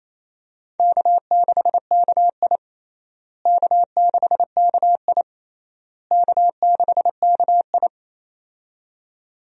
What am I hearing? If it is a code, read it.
Morse code, heard as a keyed tone, about 28 wpm, 700 Hz. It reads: K6KI K6KI K6KI
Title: USA Call Signs - 4 Characters - Repeated 3x - 28wpm